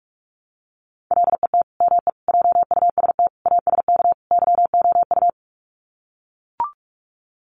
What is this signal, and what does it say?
Morse code, heard as a keyed tone, about 44 wpm, 700 Hz. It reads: LET ME JUST ASK YOU
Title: Sets of 5 Words Encore2 45wpm